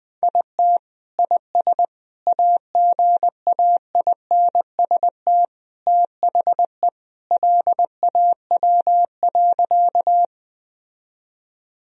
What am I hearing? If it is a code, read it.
Morse code, heard as a keyed tone, about 20 wpm, 700 Hz. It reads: IT IS AGAINST THE LAW.
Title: Sentences from Top 1000 Words - Review Part 3 - 20wpm